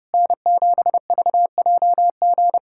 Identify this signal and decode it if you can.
Morse code, heard as a keyed tone, about 32 wpm, 700 Hz. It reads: N7VJG
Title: USA Callsign DE Callsign - 5 Characters - 30wpm